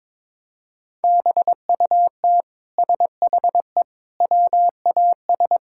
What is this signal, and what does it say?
Morse code, heard as a keyed tone, about 22 wpm, 700 Hz. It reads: BUT SHE WAS
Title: Sentences from Top 1000 Words - Review Part 5 - 22wpm